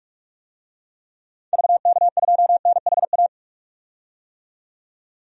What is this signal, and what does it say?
Morse code, heard as a keyed tone, about 46 wpm, 700 Hz. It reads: VK2NHA
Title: DXCC-100 Call Signs - 6 Characters - 45wpm